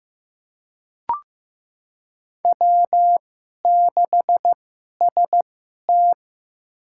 Morse code, heard as a keyed tone, about 15 wpm, spaced as for 11 wpm, 700 Hz. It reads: W6ST
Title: USA Call Signs - 4 Characters - 2x Character Spacing - 15wpm